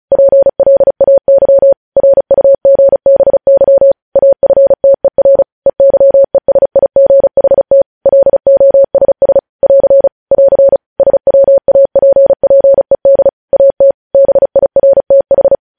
Morse code, heard as a keyed tone, about 36 wpm, 550 Hz. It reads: PLAY RUGBY AFTER EYESIGHT LOSS <AR> <AR> SWAPPED AT BIRTH
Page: news headlines in morse code at 35 wpm